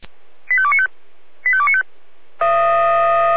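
Five-tone selective call (ZVEI), heard as quick runs of five tones, tones of about 70 ms. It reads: 85295 85295
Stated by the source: Five-tone ZVEI selective call received on an SDR.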